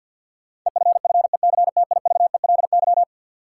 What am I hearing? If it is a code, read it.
Morse code, heard as a keyed tone, about 53 wpm, 700 Hz. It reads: EFFECTIVELY
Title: Sentences from Top 1400 Words - New Word Drill - 50wpm